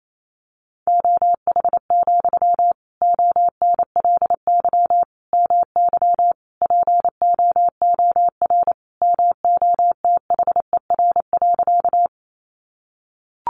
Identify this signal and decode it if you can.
Morse code, heard as a keyed tone, about 28 wpm, 700 Hz. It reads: OH, ONLY MY POOR MOTHER.
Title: Sentences from Top 1000 Words 28wpm